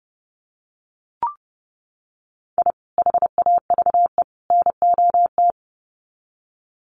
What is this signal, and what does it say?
Morse code, heard as a keyed tone, about 30 wpm, 700 Hz. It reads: I HAVE NOT
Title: Sets of 3 Words 30wpm